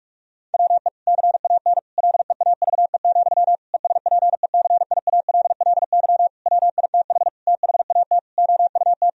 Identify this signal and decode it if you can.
Morse code, heard as a keyed tone, about 41 wpm, 700 Hz. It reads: WE CAN LEAVE, ESPECIALLY WITH THAT OUT
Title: Sentences from Top 1000 Words - Review Part 4 - 45wpm